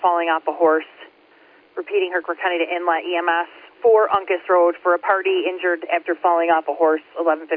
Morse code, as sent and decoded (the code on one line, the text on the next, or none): none